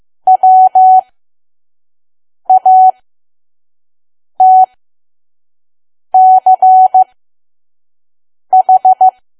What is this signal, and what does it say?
Morse code, heard as a keyed tone, about 15 wpm, spaced as for 5 wpm, 750 Hz. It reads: WATCH
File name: Archive / 5 WPM files / PM